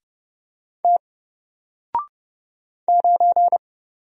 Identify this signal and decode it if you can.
Morse code, heard as a keyed tone, about 30 wpm, 700 Hz. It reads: T 9
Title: Single Letter-Number Alt 30wpm